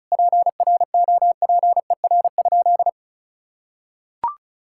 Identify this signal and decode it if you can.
Morse code, heard as a keyed tone, about 34 wpm, 700 Hz. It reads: PROPER?
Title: Sentences from Top 1000 Words - Review Part 5 - 35wpm